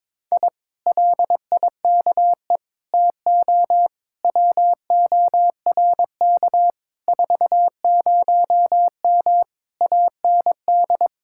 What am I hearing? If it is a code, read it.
Morse code, heard as a keyed tone, about 22 wpm, 700 Hz. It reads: I LIKE TO WORK 40M AND
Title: Speed-Racing Common QSO Phrases - 22wpm